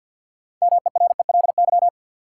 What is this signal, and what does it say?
Morse code, heard as a keyed tone, about 53 wpm, 700 Hz. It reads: MERELY
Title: Sentences from Top 1400 Words - New Word Drill - 50wpm